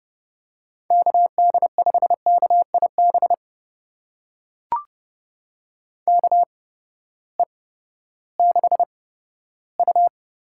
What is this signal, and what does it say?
Morse code, heard as a keyed tone, about 30 wpm, spaced as for 15 wpm, 700 Hz. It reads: KD5KIB KE6U
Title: USA Call Signs - 6 Characters - 8x Character Spacing - 30wpm